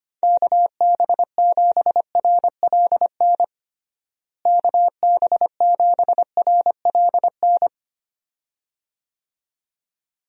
Morse code, heard as a keyed tone, about 25 wpm, 700 Hz. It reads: KB7RLN KB7RLN
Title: USA Call Signs - 6 Characters - Repeated 2x - 25wpm